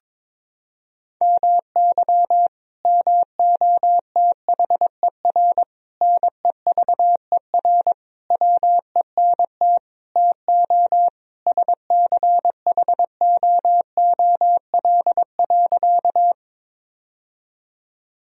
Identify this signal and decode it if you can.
Morse code, heard as a keyed tone, about 22 wpm, 700 Hz. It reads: MY MOTHER NEVER WENT TO SCHOOL.
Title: Sentences from Top 1000 Words - Review Part 1 - 22wpm